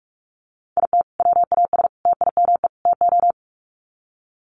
Morse code, heard as a keyed tone, about 44 wpm, 700 Hz. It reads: IT WAS TIME TO
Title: Sets of 4 Words 45wpm